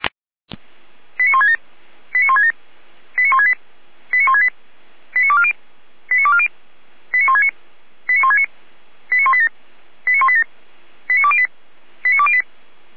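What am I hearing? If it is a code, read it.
Five-tone selective call (ZVEI), heard as quick runs of five tones, tones of about 70 ms. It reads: 89167 89167 89168 89168 89240 89240 89169 89169 89187 89187 89208 89208
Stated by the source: Five-tone ZVEI selective call received by SDR, from feuerwehr-orlamuende.org fire brigade audio archive.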